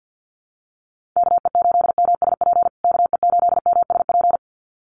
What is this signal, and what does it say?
Morse code, heard as a keyed tone, about 50 wpm, 700 Hz. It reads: KE8MSP KE8MSP
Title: USA Call Signs Repeated 2x 50wpm